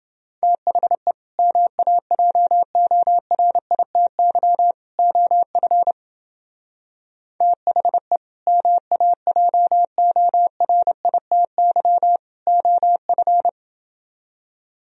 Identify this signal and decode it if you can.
Morse code, heard as a keyed tone, about 28 wpm, 700 Hz. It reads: THE MAJORITY OF THE MAJORITY OF